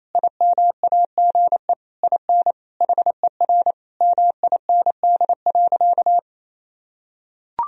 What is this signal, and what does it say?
Morse code, heard as a keyed tone, about 28 wpm, 700 Hz. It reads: IMAGE IN HER MIND.
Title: Sentences from Top 1000 Words - Review Part 4 - 28wpm